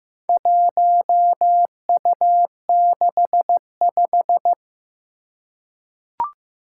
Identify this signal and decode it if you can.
Morse code, heard as a keyed tone, about 15 wpm, 700 Hz. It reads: 1U65
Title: Four Challenging Letter-Number Combinations 15wpm